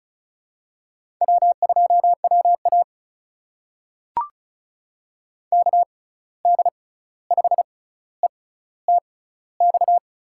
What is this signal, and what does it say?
Morse code, heard as a keyed tone, about 36 wpm, spaced as for 16 wpm, 700 Hz. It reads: W2WA KD5ETX
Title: USA Call Signs - 6x Character Spacing - 35wpm